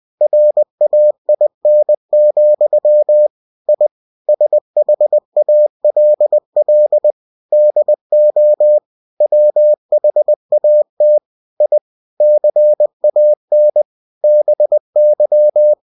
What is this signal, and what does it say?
Morse code, heard as a keyed tone, about 20 wpm, 600 Hz. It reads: RAIN, I SHALL DO WHAT I CAN BY